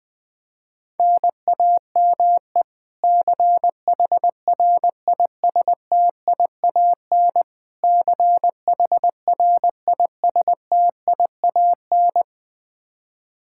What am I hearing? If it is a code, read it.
Morse code, heard as a keyed tone, about 21 wpm, 700 Hz. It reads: NAME CHRISTIAN CHRISTIAN
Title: Practice QSO - Exchange 1 - Beginning - 20wpm